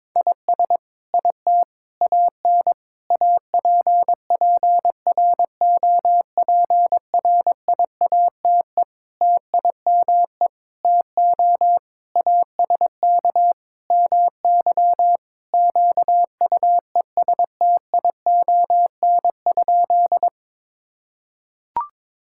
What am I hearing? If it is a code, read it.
Morse code, heard as a keyed tone, about 22 wpm, 700 Hz. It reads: IS IT AN APPROPRIATE TIME TO ASK MY QUESTION?
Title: Sentences from Top 1000 Words - Review Part 5 - 22wpm